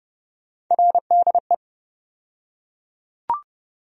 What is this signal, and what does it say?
Morse code, heard as a keyed tone, about 30 wpm, 700 Hz. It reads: RDE
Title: USA Call Sign Postfixes - 3 Letters 30wpm